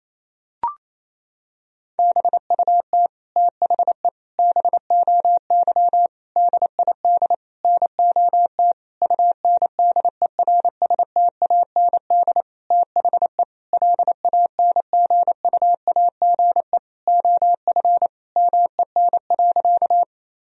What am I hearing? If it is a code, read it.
Morse code, heard as a keyed tone, about 28 wpm, 700 Hz. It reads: BUT THE BOY DID NOT UNDERSTAND THE LANGUAGE OF MEN.
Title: Sentences from Top 1000 Words - Review Part 3 - 28wpm